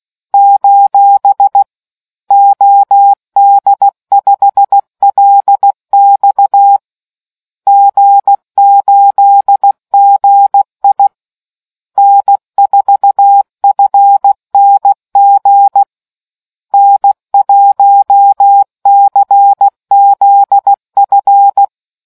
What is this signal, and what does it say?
Morse code, heard as a keyed tone, about 16 wpm, 800 Hz. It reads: : OD5LX G8GI N4FNG N1CZF